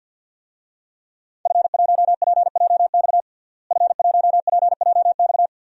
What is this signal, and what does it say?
Morse code, heard as a keyed tone, about 46 wpm, 700 Hz. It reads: F1PJX F1PJX
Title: DXCC-100 Call Signs - 5 Characters - Repeated 3x - 50wpm